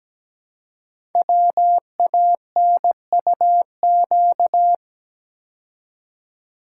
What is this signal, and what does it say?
Morse code, heard as a keyed tone, about 17 wpm, 700 Hz. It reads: WANUQ
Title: Code Groups - 5 Letter:Numbers - 17wpm